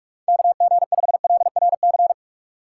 Morse code, heard as a keyed tone, about 42 wpm, 700 Hz. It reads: KG5LRC
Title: USA Call Signs - 3x Character Spacing - 45wpm